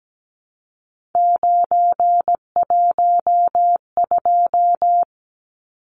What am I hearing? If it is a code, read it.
Morse code, heard as a keyed tone, about 17 wpm, 700 Hz. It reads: 912
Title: Numbers - 3-Digit 17wpm